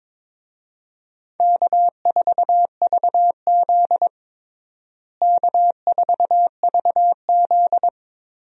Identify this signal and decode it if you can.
Morse code, heard as a keyed tone, about 22 wpm, 700 Hz. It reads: K4VZ K4VZ
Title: USA Call Signs - 4 Characters - Repeated 3x - 22wpm